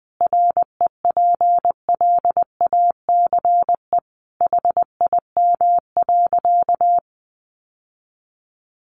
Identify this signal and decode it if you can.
Morse code, heard as a keyed tone, about 20 wpm, 700 Hz. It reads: REPLACE HIM.
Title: Sentences from Top 1400 Words 20wpm